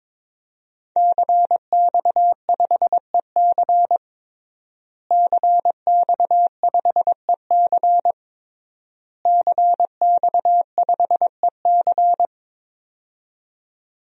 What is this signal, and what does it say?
Morse code, heard as a keyed tone, about 22 wpm, 700 Hz. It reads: CX5EC CX5EC CX5EC
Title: DXCC-100 Call Signs - 5 Characters - Repeated 3x - 22wpm